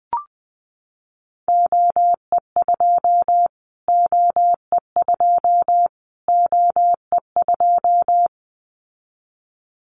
Morse code, heard as a keyed tone, about 20 wpm, 700 Hz. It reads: OE2 OE2 OE2
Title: Callsign Prefixes for Top 25 DXCC - Repeated 3x - 20wpm